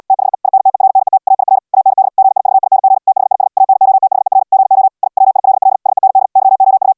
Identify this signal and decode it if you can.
Morse code, heard as a keyed tone, about 80 wpm, 770 Hz. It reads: THE INTERNET AIR AND GIVING THIS ANOTHER GO I WILL SEND YOU